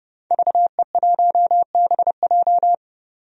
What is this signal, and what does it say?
Morse code, heard as a keyed tone, about 30 wpm, 700 Hz. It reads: VE1BJ
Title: DXCC-100 Call Signs - 5 Characters - 30wpm